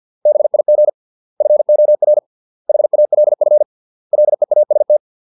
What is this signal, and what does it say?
Morse code, heard as a keyed tone, about 50 wpm, 600 Hz. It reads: BIG FOR HALF LEAST